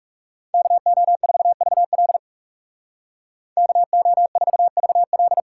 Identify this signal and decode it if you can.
Morse code, heard as a keyed tone, about 36 wpm, 700 Hz. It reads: KO4VL KO4VL
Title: Speed Racing - USA 2x2 Call Signs 30wpm